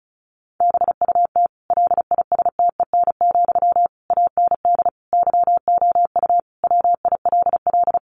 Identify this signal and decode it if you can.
Morse code, heard as a keyed tone, about 34 wpm, 700 Hz. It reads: BUT LISTEN, AND YOU WILL
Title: Sentences from Top 600 Words 35wpm